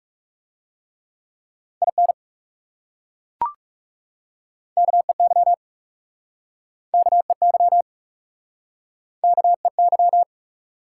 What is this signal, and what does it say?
Morse code, heard as a keyed tone, about 40 wpm, 700 Hz. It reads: IN KEY KEY KEY